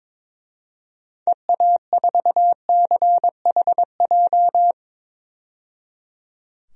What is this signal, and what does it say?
Morse code, heard as a keyed tone, about 22 wpm, 700 Hz. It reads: EA4CHJ
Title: DXCC-100 Call Signs - 6 Characters - 22wpm